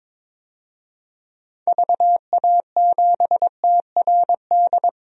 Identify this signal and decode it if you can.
Morse code, heard as a keyed tone, about 22 wpm, 700 Hz. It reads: VA7TRD
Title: DXCC-100 Call Signs - 6 Characters - Repeated 2x - 22wpm